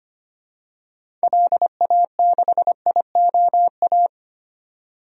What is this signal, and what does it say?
Morse code, heard as a keyed tone, about 25 wpm, 700 Hz. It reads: LA6IOA
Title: DXCC-100 Call Signs - 6 Characters - 25wpm